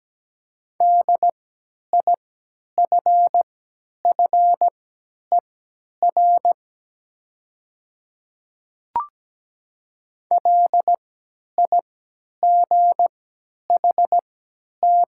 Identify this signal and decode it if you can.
Morse code, heard as a keyed tone, about 17 wpm, spaced as for 10 wpm, 690 Hz. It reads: DIFFER LIGHT
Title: 3x Character Spacing - Top 200 Words - 17wpm